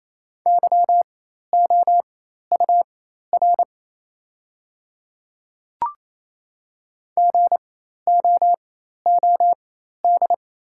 Morse code, heard as a keyed tone, about 28 wpm, spaced as for 13 wpm, 690 Hz. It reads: YOUR GOOD
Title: 4x Character Spacing - Top 200 Words - 28wpm